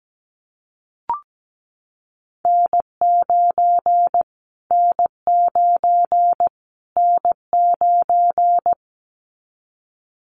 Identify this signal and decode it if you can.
Morse code, heard as a keyed tone, about 17 wpm, 700 Hz. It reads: N9 N9 N9
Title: Callsign Prefixes for Top 25 DXCC - Repeated 3x - 17wpm